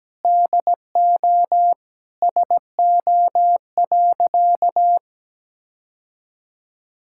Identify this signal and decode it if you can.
Morse code, heard as a keyed tone, about 17 wpm, 700 Hz. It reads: DO SO.
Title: Sentences from Top 1000 Words - Review Part 6 - 17wpm